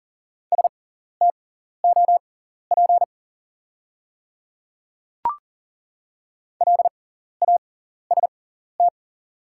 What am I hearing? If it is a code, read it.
Morse code, heard as a keyed tone, about 40 wpm, spaced as for 14 wpm, 690 Hz. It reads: STOP LAST
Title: 6x Character Spacing - Top 300 Words - 40wpm